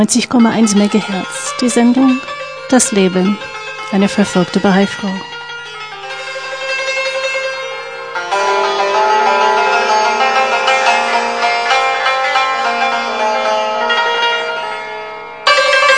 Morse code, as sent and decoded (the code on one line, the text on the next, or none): none